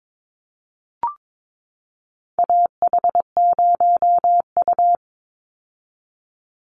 Morse code, heard as a keyed tone, about 22 wpm, 700 Hz. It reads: AH0U
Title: Call Signs - 22wpm